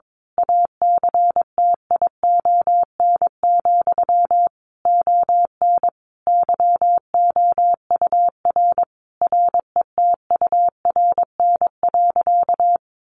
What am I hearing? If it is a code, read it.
Morse code, heard as a keyed tone, about 22 wpm, 700 Hz. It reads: ACTION, ON YOUR RETURN.
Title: A Sound Of Thunder 22wpm